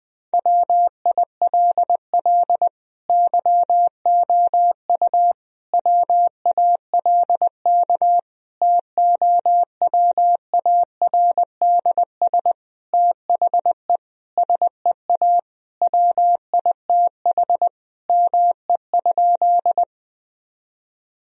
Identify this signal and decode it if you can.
Morse code, heard as a keyed tone, about 20 wpm, 700 Hz. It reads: WILL YOU WALK TOWARDS THE SEA WITH ME?
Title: Sentences from Top 1000 Words - Review Part 2 - 20wpm